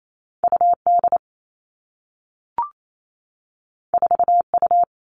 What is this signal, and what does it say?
Morse code, heard as a keyed tone, about 28 wpm, 700 Hz. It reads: UD 4U
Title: All Cut Numbers - 2-Digit 28wpm